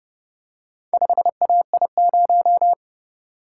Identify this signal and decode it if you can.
Morse code, heard as a keyed tone, about 29 wpm, 700 Hz. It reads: 5AI0